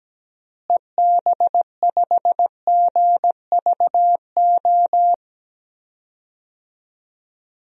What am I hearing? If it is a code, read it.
Morse code, heard as a keyed tone, about 17 wpm, 700 Hz. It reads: EB5GVO
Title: DXCC-100 Call Signs - 6 Characters - Repeated 3x - 17wpm